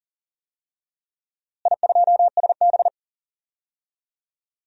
Morse code, heard as a keyed tone, about 40 wpm, 700 Hz. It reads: I2SB
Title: DXCC-100 Call Signs - 4 Characters - Repeated 3x - 40wpm